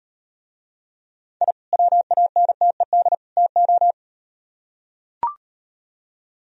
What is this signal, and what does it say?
Morse code, heard as a keyed tone, about 38 wpm, 700 Hz. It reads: I WANTED TO